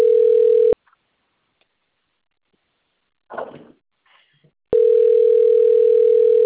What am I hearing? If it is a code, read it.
Ringback tone (North America), 440 + 480 Hz.